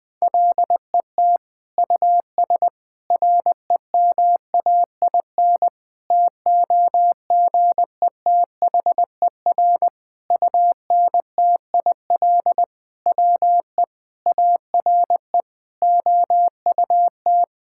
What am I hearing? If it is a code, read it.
Morse code, heard as a keyed tone, about 20 wpm, 700 Hz. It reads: LET US REMAIN TOGETHER UNTIL WE ARE OUT
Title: Sentences from Top 1600 Words 20wpm